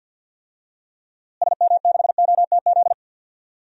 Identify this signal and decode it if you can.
Morse code, heard as a keyed tone, about 50 wpm, 700 Hz. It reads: SM6OTZ
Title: DXCC-100 Call Signs - 6 Characters - Repeated 2x - 50wpm